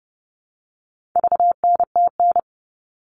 Morse code, heard as a keyed tone, about 30 wpm, 700 Hz. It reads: VNTN
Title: All Cut Numbers - 4-Digit 30wpm